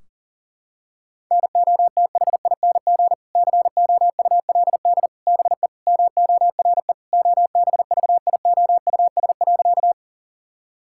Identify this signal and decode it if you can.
Morse code, heard as a keyed tone, about 40 wpm, 700 Hz. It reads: NOTHING COULD BE MORE OBVIOUS.
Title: Sentences from Top 1600 Words 40wpm